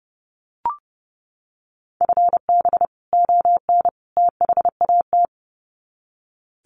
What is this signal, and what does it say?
Morse code, heard as a keyed tone, about 29 wpm, 700 Hz. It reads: FB ON THAT
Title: Common QSO Phrases 30wpm